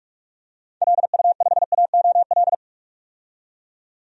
Morse code, heard as a keyed tone, about 43 wpm, 700 Hz. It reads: LU5AOL
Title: DXCC-100 Call Signs - 6 Characters - 45wpm